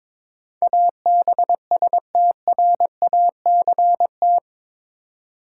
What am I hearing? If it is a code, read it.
Morse code, heard as a keyed tone, about 22 wpm, 700 Hz. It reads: ABSTRACT
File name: practice / 22wpm / Eight Letter Words - 22wpm